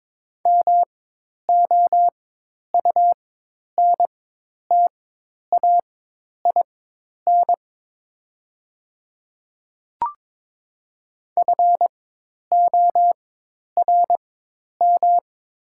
Morse code, heard as a keyed tone, about 22 wpm, spaced as for 10 wpm, 700 Hz. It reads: MOUNTAIN FORM
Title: 4x Character Spacing - Top 300 Words - 22wpm